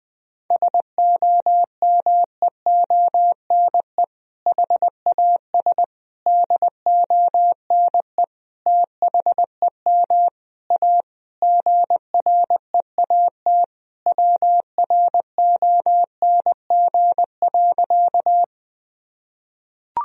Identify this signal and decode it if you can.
Morse code, heard as a keyed tone, about 20 wpm, 700 Hz. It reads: SOMEONE HAS DONE THEM A GREAT WRONG.
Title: Sentences from Top 1000 Words - Review Part 4 - 20wpm